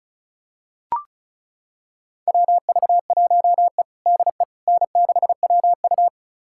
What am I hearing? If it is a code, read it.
Morse code, heard as a keyed tone, about 36 wpm, 700 Hz. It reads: WV1E DE N6WU
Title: USA Callsign DE Callsign - 4 Characters - 35wpm